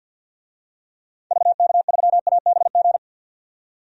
Morse code, heard as a keyed tone, about 49 wpm, 700 Hz. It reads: VK3ABG
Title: DXCC-100 Call Signs - 6 Characters - 50wpm